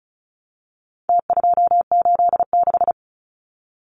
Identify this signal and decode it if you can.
Morse code, heard as a keyed tone, about 34 wpm, 700 Hz. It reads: T286
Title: Numbers - Contest Style - 4-Digit 35wpm